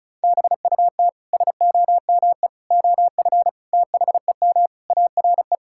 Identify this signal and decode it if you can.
Morse code, heard as a keyed tone, about 36 wpm, 700 Hz. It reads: BUT SOME OF THEM ARE
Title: Sentences from Top 1600 Words 35wpm